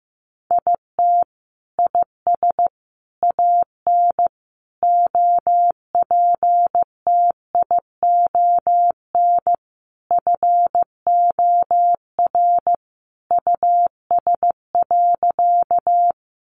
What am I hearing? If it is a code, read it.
Morse code, heard as a keyed tone, about 14 wpm, 700 Hz. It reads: IT IS AN OPTION FOR US.
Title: Sentences from Top 1300 Words 15wpm